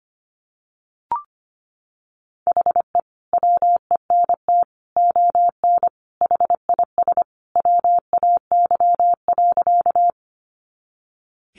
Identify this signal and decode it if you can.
Morse code, heard as a keyed tone, about 25 wpm, 700 Hz. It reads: HE WENT ON HIS WAY.
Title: Sentences from Top 200 Words 25wpm